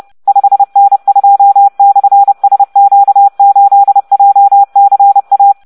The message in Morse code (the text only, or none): 5N2/SQ8JCA